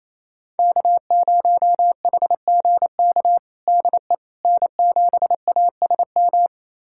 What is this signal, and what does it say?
Morse code, heard as a keyed tone, about 28 wpm, 700 Hz. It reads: K0HGK DE N7ASM
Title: USA Callsign DE Callsign - 5 Characters - 28wpm